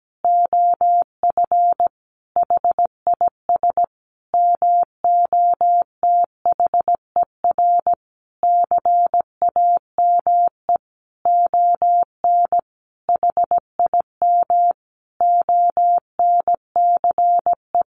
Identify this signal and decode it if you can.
Morse code, heard as a keyed tone, about 17 wpm, 700 Hz. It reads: OF HIS MOTHER CAME ON HIM ONCE
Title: Sentences from Top 1100 Words 17wpm